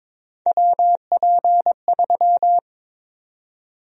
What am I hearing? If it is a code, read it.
Morse code, heard as a keyed tone, about 21 wpm, 700 Hz. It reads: WP3
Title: USA Call Sign Prefixes - 22wpm